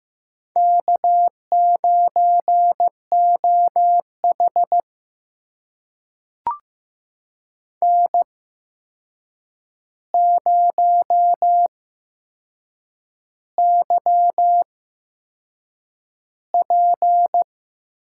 Morse code, heard as a keyed tone, about 15 wpm, spaced as for 6 wpm, 700 Hz. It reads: K9OH N0YP
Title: USA Call Signs - 4 Characters - 8x Character Spacing - 15wpm